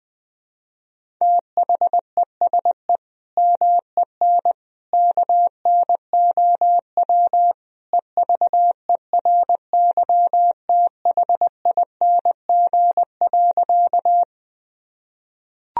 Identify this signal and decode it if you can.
Morse code, heard as a keyed tone, about 20 wpm, 700 Hz. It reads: THESE MEN KNOW EVERYTHING.
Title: Sentences from Top 1000 Words - Review Part 1 - 20wpm